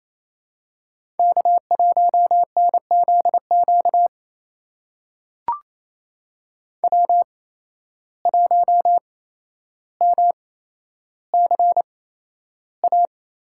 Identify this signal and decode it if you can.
Morse code, heard as a keyed tone, about 27 wpm, spaced as for 12 wpm, 700 Hz. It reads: K1NZQ W1MCA
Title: USA Call Signs - 5 Characters - 8x Character Spacing - 28wpm